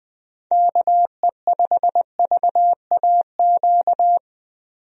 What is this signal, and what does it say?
Morse code, heard as a keyed tone, about 20 wpm, 700 Hz. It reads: KE5VAQ
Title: USA Call Signs - 6 Characters - 20wpm